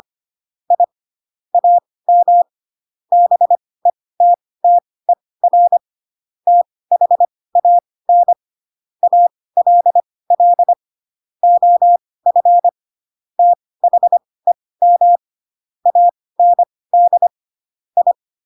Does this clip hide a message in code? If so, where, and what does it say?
Yes: Morse code from 0.7 to 18.1 s, heard as a keyed tone, about 25 wpm, spaced as for 18 wpm, 700 Hz. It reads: I AM BETTER THAN ALL OF THEM AND I